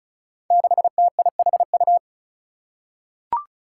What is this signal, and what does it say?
Morse code, heard as a keyed tone, about 36 wpm, 700 Hz. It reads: 6TIHU
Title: Five Challenging Letter-Number Combinations 35wpm